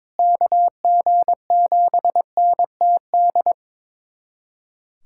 Morse code, heard as a keyed tone, about 22 wpm, 700 Hz. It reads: KG7NTD
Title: USA Callsign DE Callsign - 6 Characters - 22wpm